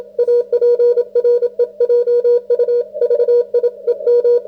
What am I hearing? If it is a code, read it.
Morse code, heard as a keyed tone, about 28 wpm, 530 Hz. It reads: APREJU VI EM